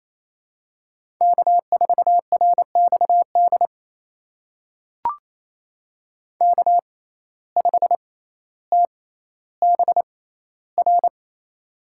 Morse code, heard as a keyed tone, about 28 wpm, spaced as for 14 wpm, 700 Hz. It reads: K4RXD K5TBR
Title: USA Call Signs - 5 Characters - 6x Character Spacing - 28wpm